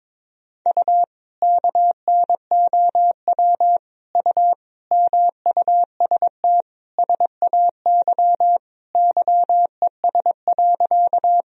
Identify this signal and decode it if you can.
Morse code, heard as a keyed tone, about 22 wpm, 700 Hz. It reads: U KNOW U MUST SAY YES.
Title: Sentences from Top 500 Words and Abbreviations 22wpm